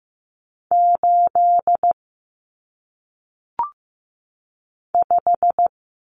Morse code, heard as a keyed tone, about 15 wpm, 700 Hz. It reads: 8 5
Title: Numbers - 1-Digit 15wpm